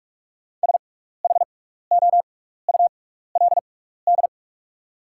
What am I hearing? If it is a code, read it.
Morse code, heard as a keyed tone, about 44 wpm, spaced as for 15 wpm, 700 Hz. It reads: SHOULD
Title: 6x Character Spacing - Top 300 Words - 45wpm